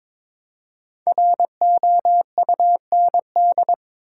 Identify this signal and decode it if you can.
Morse code, heard as a keyed tone, about 22 wpm, 700 Hz. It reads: ROUND
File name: 5 Letter Words - First 3 Letters Given - 22wpm